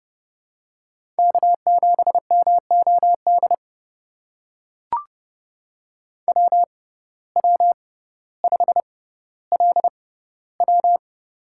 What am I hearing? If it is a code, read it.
Morse code, heard as a keyed tone, about 30 wpm, spaced as for 15 wpm, 700 Hz. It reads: K7MOD WW5LW
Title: USA Call Signs - 6x Character Spacing - 30wpm